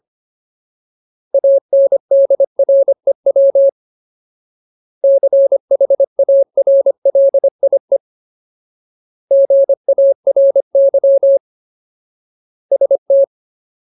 Morse code, heard as a keyed tone, about 25 wpm, 550 Hz. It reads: ANDREW CHARLIE GARY ST
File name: Ultimate Word List / 25WPM Ultimate